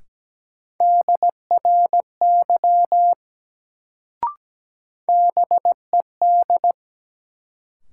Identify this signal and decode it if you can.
Morse code, heard as a keyed tone, about 17 wpm, 700 Hz. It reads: DRY BED
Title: Sentences from Top 500 Words - New Word Drill - 17wpm